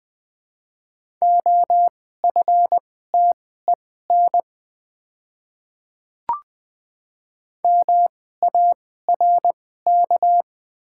Morse code, heard as a keyed tone, about 20 wpm, spaced as for 14 wpm, 700 Hz. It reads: OFTEN MARK
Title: 2x Character Spacing - Top 300 Words - 20wpm